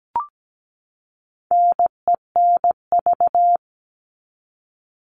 Morse code, heard as a keyed tone, about 17 wpm, 700 Hz. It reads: NENV
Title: All Cut Numbers - 4-Digit 17wpm